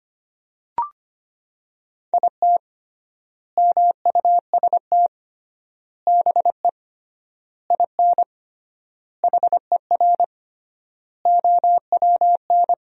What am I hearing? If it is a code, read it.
Morse code, heard as a keyed tone, about 25 wpm, 700 Hz. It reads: IT MUST BE IN HER OWN
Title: 3x Word-Spacing - Sentences from Top 300 Words - 25wpm